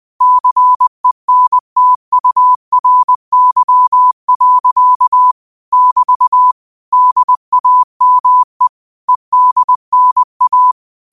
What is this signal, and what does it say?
Morse code, heard as a keyed tone, about 20 wpm, 1000 Hz. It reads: CENTURY. <BT> DAME EDNA